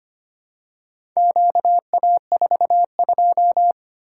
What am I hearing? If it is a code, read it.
Morse code, heard as a keyed tone, about 25 wpm, 700 Hz. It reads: QA42
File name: practice / 25wpm / Four Letter-Number Combinations - 25wpm